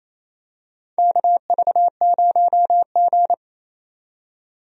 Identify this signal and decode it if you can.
Morse code, heard as a keyed tone, about 28 wpm, 700 Hz. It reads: KV0G
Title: USA Call Signs - 4 Characters - 28wpm